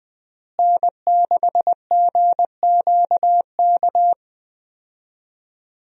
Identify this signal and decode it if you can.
Morse code, heard as a keyed tone, about 20 wpm, 700 Hz. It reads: N6GQK
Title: USA Call Signs - 5 Characters - Repeated 2x - 20wpm